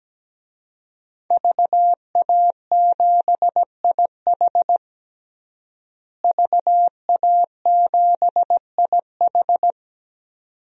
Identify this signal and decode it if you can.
Morse code, heard as a keyed tone, about 17 wpm, 700 Hz. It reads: VA7IH VA7IH
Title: DXCC-100 Call Signs - 5 Characters - Repeated 2x - 17wpm